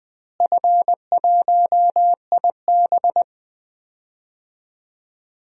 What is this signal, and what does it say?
Morse code, heard as a keyed tone, about 20 wpm, 700 Hz. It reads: F1IB
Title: DXCC-100 Call Signs - 4 Characters - Repeated 3x - 20wpm